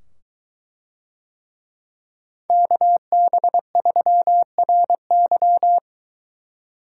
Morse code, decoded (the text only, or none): KB3RY